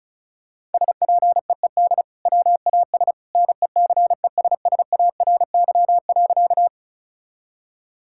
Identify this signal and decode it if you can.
Morse code, heard as a keyed tone, about 36 wpm, 700 Hz. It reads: SPEED WAS NECESSARY.
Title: Sentences from Top 1000 Words - Review Part 5 - 35wpm